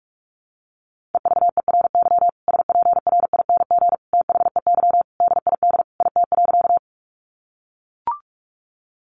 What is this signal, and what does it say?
Morse code, heard as a keyed tone, about 44 wpm, 700 Hz. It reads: EVERY SPRING THEY DID IT.
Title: Sentences from Top 1200 Words 45wpm